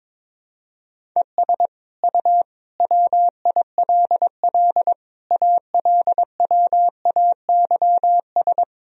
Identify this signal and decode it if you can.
Morse code, heard as a keyed tone, about 22 wpm, 700 Hz. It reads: ES U WILL ALWAYS
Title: Sentences from Top 400 Words and Abbreviations 22wpm